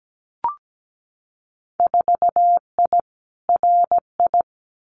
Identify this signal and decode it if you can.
Morse code, heard as a keyed tone, about 17 wpm, 700 Hz. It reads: 4I RI
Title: Winter Field Day Exchanges Encore 17wpm